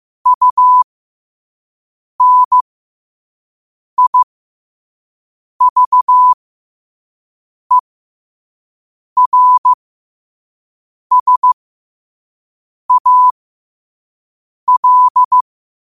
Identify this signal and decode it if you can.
Morse code, heard as a keyed tone, about 15 wpm, spaced as for 5 wpm, 1000 Hz. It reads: UNIVERSAL
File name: qotdfiles / 5wpm